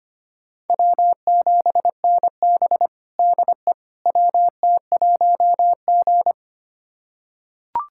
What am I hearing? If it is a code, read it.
Morse code, heard as a keyed tone, about 25 wpm, 700 Hz. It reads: W7NB DE WT1G
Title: USA Callsign DE Callsign - 4 Characters - 25wpm